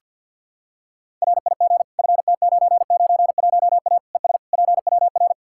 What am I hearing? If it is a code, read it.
Morse code, heard as a keyed tone, about 46 wpm, 700 Hz. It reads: RIG FT991A ES PWR
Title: Practice QSO - Exchange 2 - Info - 50wpm